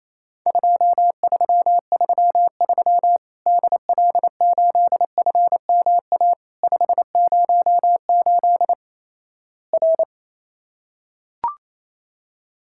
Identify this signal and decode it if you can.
Morse code, heard as a keyed tone, about 28 wpm, 700 Hz. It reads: 2333 DL8FMA 508 R
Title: Worked All Europe DX Contest - QTC Exchange - 28wpm